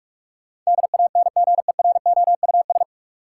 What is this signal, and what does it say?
Morse code, heard as a keyed tone, about 46 wpm, 700 Hz. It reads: DANGEROUS
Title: Sentences from Top 1200 Words - New Word Drill - 45wpm